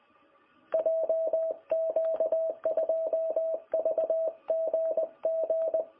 Morse code, heard as a keyed tone, about 19 wpm, 660 Hz. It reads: JQ2VGG
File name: jh8cbh / 2021cw